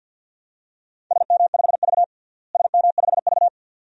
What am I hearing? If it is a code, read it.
Morse code, heard as a keyed tone, about 50 wpm, 700 Hz. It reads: SM5V SM5V
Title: DXCC-100 Call Signs - 4 Characters - Repeated 2x - 50wpm